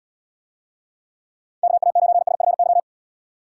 Morse code, heard as a keyed tone, about 78 wpm, 700 Hz. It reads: KA9SFY